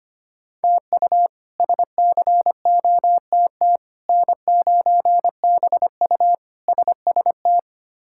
Morse code, heard as a keyed tone, about 25 wpm, 700 Hz. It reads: TU SCOTT N9BU SST
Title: K1USN SST QSO Practice - 25wpm